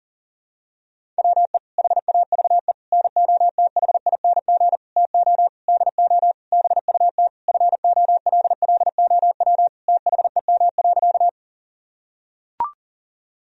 Morse code, heard as a keyed tone, about 40 wpm, 700 Hz. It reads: WE HAVE NOTHING TO DO BUT FOLLOW THEM.
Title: Sentences from Top 1000 Words - Review Part 1 - 40wpm